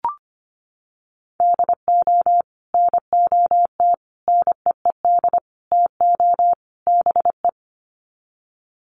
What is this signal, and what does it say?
Morse code, heard as a keyed tone, about 25 wpm, 700 Hz. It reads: DO NOT NEED TO BE
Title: Sets of 5 Words Encore 25wpm